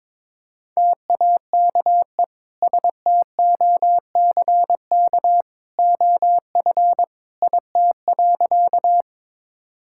Morse code, heard as a keyed tone, about 22 wpm, 700 Hz. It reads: TAKE STOCK OF IT.
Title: Sentences from Top 1000 Words - Review Part 5 - 22wpm